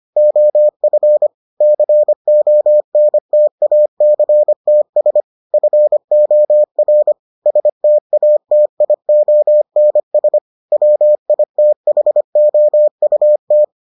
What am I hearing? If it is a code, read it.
Morse code, heard as a keyed tone, about 25 wpm, 600 Hz. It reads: OF CONTACTS FOR STATIONS WITHOUT